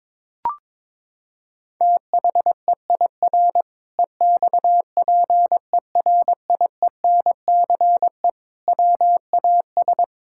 Morse code, heard as a keyed tone, about 21 wpm, 700 Hz. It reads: THEIR EXPERIENCE WAS
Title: Sentences from Top 1000 Words - Review Part 5 - 22wpm